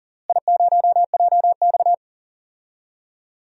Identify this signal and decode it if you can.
Morse code, heard as a keyed tone, about 38 wpm, 700 Hz. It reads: I0JX
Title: DXCC-100 Call Signs - 4 Characters - Repeated 3x - 40wpm